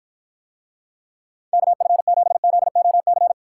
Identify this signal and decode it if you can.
Morse code, heard as a keyed tone, about 54 wpm, 700 Hz. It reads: KF7ZOC